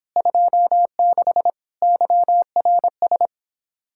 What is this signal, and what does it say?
Morse code, heard as a keyed tone, about 29 wpm, 700 Hz. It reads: 26 YRS